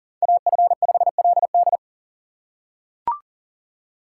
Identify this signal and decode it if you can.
Morse code, heard as a keyed tone, about 40 wpm, 700 Hz. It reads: AF5LD
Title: USA Call Signs - 5 Characters - 6x Character Spacing - 40wpm